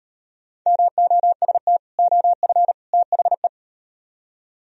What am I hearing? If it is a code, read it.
Morse code, heard as a keyed tone, about 38 wpm, 700 Hz. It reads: MOST OF THE